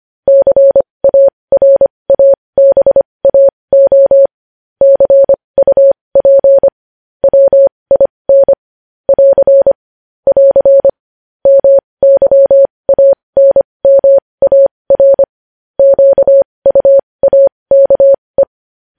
Morse code, heard as a keyed tone, about 25 wpm, spaced as for 20 wpm, 550 Hz. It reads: CARABAO CUP WIN <AR> <AR> MYANMAR QUAKE
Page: News Headlines in Morse Code at 20 WPM